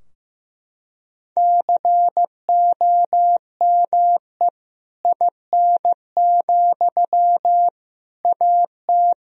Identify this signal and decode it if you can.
Morse code, heard as a keyed tone, about 15 wpm, 700 Hz. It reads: COME IN, AT
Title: Sentences from Top 1000 Words - Review Part 4 - 15wpm